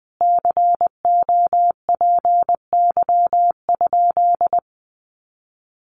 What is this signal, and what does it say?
Morse code, heard as a keyed tone, about 20 wpm, 700 Hz. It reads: COPY?
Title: Sentences from Top 1400 Words 20wpm